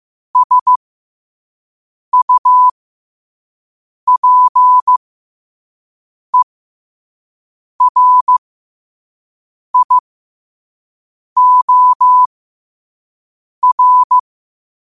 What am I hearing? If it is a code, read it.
Morse code, heard as a keyed tone, about 15 wpm, spaced as for 5 wpm, 1000 Hz. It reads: SUPERIOR
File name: qotdfiles / 5wpm